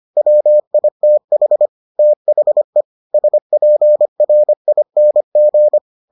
Morse code, heard as a keyed tone, about 25 wpm, 600 Hz. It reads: WITH THE SPRING